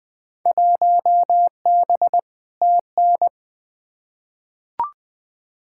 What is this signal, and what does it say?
Morse code, heard as a keyed tone, about 21 wpm, 700 Hz. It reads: 1B TN
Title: Field Day Exchanges - 20wpm